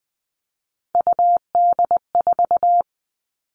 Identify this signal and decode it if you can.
Morse code, heard as a keyed tone, about 20 wpm, 700 Hz. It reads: UD4
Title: All Cut Numbers - 3-Digit 20wpm